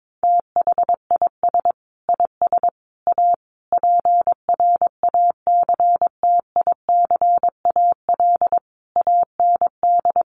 Morse code, heard as a keyed tone, about 22 wpm, 700 Hz. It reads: THIS IS A PRACTICAL AND
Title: Sentences from Top 1400 Words 22wpm